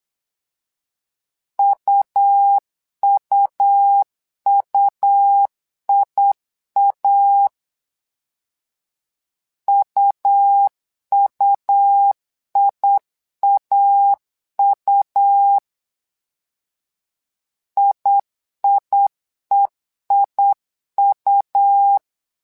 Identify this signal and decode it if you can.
Morse code, heard as a keyed tone, about 8 wpm, 800 Hz. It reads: UUUIA UUIAU IIEIU
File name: LEZIONI / 2 UHDN / 2 U 8wpm 2028